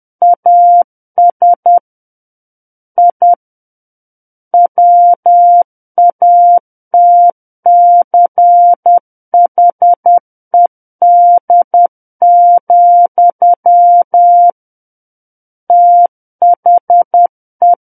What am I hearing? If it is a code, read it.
Morse code, heard as a keyed tone, about 10 wpm, 700 Hz. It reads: AS I WATCHED, THE
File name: Book One / 01-Chapter 1 - 10 WPM